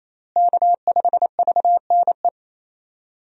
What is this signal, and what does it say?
Morse code, heard as a keyed tone, about 28 wpm, 700 Hz. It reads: K5VNE
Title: USA Call Signs - 5 Characters - Repeated 2x - 28wpm